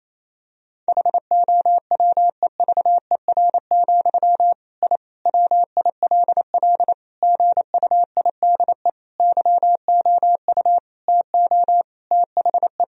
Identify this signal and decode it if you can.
Morse code, heard as a keyed tone, about 28 wpm, 700 Hz. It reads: HOWEVER, I WILL GUIDE YOU TO THE
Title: Sentences from Top 1000 Words - Review Part 3 - 28wpm